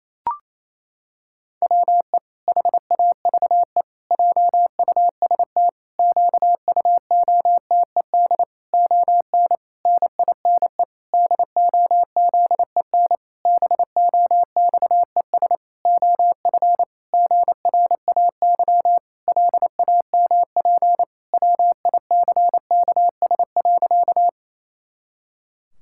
Morse code, heard as a keyed tone, about 28 wpm, 700 Hz. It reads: WE HAVE JUST QUOTED ON NINE DOZEN BOXES OF GRAY LAMP WICKS.
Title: Pangram Challenge v2 - 28wpm